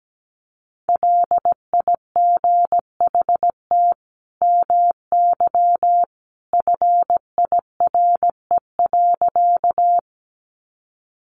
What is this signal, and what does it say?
Morse code, heard as a keyed tone, about 17 wpm, 700 Hz. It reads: LIGHT MY FIRE.
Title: Sentences from Top 400 Words 17wpm